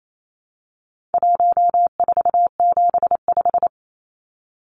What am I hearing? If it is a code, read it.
Morse code, heard as a keyed tone, about 28 wpm, 700 Hz. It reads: 1475
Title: Numbers - 4-Digit 28wpm